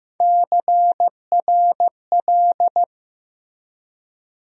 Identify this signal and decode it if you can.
Morse code, heard as a keyed tone, about 15 wpm, 700 Hz. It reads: CRL
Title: USA Call Sign Postfixes - 3 Letters 15wpm